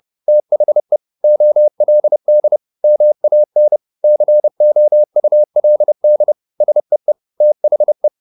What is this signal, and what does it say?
Morse code, heard as a keyed tone, about 30 wpm, 600 Hz. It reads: THE OLD MAN COULD SEE THE
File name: the old man and the sea / the old man and the sea 1 30wpm